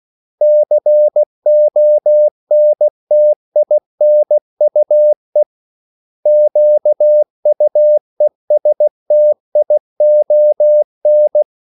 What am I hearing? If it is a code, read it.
Morse code, heard as a keyed tone, about 16 wpm, 600 Hz. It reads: CONTINUE QUESTION
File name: CW / Practice / 500MostCommonEnglishWords16